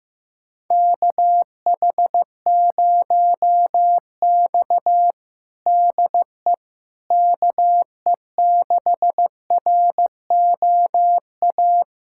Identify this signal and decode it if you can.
Morse code, heard as a keyed tone, about 15 wpm, 700 Hz. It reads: KH0X DE KE6ROA
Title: USA Callsign DE Callsign 15wpm